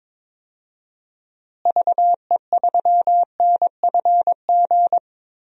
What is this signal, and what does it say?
Morse code, heard as a keyed tone, about 22 wpm, 700 Hz. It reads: VE3NFG
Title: DXCC-100 Call Signs - 6 Characters - Repeated 3x - 22wpm